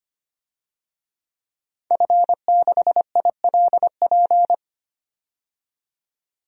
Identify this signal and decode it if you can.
Morse code, heard as a keyed tone, about 25 wpm, 700 Hz. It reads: F6ILP
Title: DXCC-100 Call Signs - 5 Characters - 25wpm